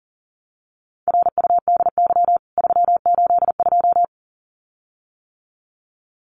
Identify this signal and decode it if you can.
Morse code, heard as a keyed tone, about 40 wpm, 700 Hz. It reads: RUDY 382
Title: CWT Exchange 40wpm